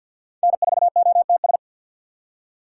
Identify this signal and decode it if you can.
Morse code, heard as a keyed tone, about 57 wpm, 700 Hz. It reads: N4OTS
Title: Speed Racing - USA 1x3 Call Signs 50wpm